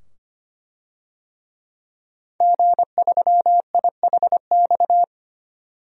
Morse code, heard as a keyed tone, about 25 wpm, 700 Hz. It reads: G3IHX
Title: DXCC-100 Call Signs - 5 Characters - 25wpm